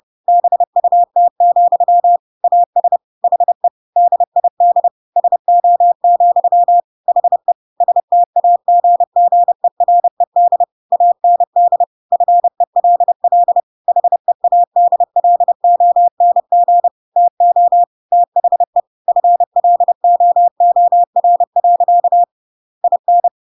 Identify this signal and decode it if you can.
Morse code, heard as a keyed tone, about 30 wpm, 700 Hz. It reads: BUT, AS HE DID SO, HE STAGGERED AND FELL HEADLONG TO THE FLOOR. IN
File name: Pinocchio / Pinocchio ch7 30 30 ik1hge